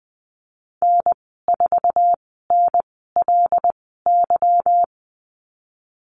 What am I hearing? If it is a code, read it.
Morse code, heard as a keyed tone, about 20 wpm, spaced as for 15 wpm, 700 Hz. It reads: N4NLY
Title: 167 Lesson 27g - Number 7 - Call Signs 20wpm